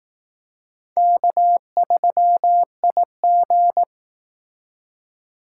Morse code, heard as a keyed tone, about 18 wpm, 700 Hz. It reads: K3IG
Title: Speed Racing - USA 1x2 Call Signs 15wpm